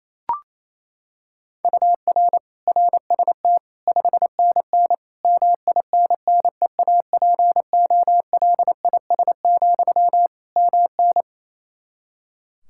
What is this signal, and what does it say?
Morse code, heard as a keyed tone, about 28 wpm, 700 Hz. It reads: UR RST 5NN MINNEAPOLIS, MN
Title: Practice QSO - Exchange 1 - Beginning - 28wpm